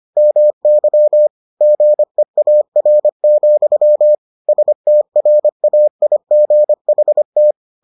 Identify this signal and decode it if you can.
Morse code, heard as a keyed tone, about 25 wpm, 600 Hz. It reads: MY GEAR, STRAIGHT